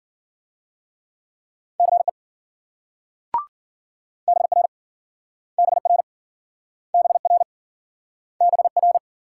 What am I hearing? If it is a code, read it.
Morse code, heard as a keyed tone, about 53 wpm, 700 Hz. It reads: CE BR BR BR BR